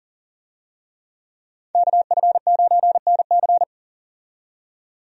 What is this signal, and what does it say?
Morse code, heard as a keyed tone, about 40 wpm, 700 Hz. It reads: KF9NC
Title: USA Call Signs - 5 Characters - 40wpm